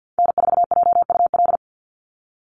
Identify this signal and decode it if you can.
Morse code, heard as a keyed tone, about 50 wpm, 700 Hz. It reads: N4JUL
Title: USA Call Signs 50wpm